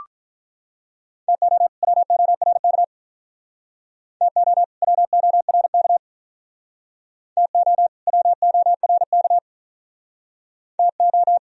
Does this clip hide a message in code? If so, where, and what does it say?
Yes: Morse code from 1.3 to 11.4 s, heard as a keyed tone, about 44 wpm, 700 Hz. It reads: TO WORK TO WORK TO WORK TO